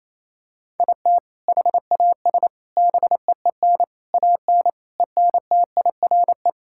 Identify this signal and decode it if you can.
Morse code, heard as a keyed tone, about 28 wpm, 700 Hz. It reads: IT HAS BEEN AN ENTIRE
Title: Sentences from Top 1000 Words - Review Part 5 - 28wpm